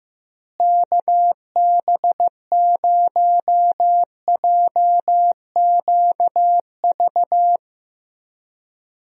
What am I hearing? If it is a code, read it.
Morse code, heard as a keyed tone, about 15 wpm, 700 Hz. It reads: KB0JQV
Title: USA Call Signs - 6 Characters - Repeated 3x - 15wpm